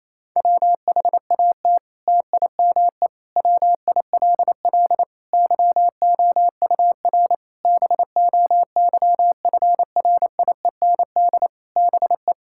Morse code, heard as a keyed tone, about 28 wpm, 700 Hz. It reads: WHAT TIME WILL YOUR BOYFRIEND BE
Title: Sentences from Top 1000 Words - Review Part 5 - 28wpm